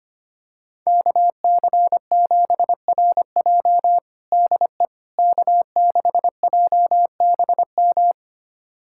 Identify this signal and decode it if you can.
Morse code, heard as a keyed tone, about 25 wpm, 700 Hz. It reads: KC7RJ DE K6JBM
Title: USA Callsign DE Callsign - 5 Characters - 25wpm